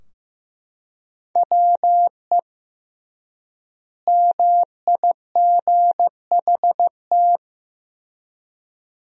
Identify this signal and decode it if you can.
Morse code, heard as a keyed tone, about 15 wpm, 700 Hz. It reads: WE MIGHT
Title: 3x Word-Spacing - Sentences from Top 300 Words - 15wpm